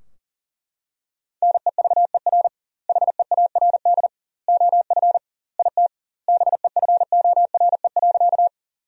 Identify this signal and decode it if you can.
Morse code, heard as a keyed tone, about 40 wpm, 700 Hz. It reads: NEVER HEARD OF IT BEFORE.
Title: 2x Word-Spacing - Sentences from Top 400 Words - 40wpm